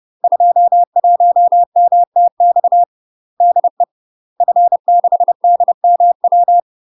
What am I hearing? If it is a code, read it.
Morse code, heard as a keyed tone, about 30 wpm, 700 Hz. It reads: 21MTX DE F6DMW
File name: UK MASTER QSOs TRAINING/30WPM